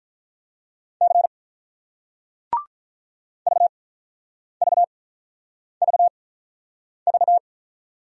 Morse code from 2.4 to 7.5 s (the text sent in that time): VVVV